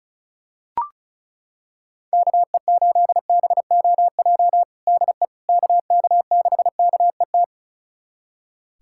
Morse code, heard as a keyed tone, about 36 wpm, 700 Hz. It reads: KE8BOJ DE KK6KET
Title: USA Callsign DE Callsign - 6 Characters - 35wpm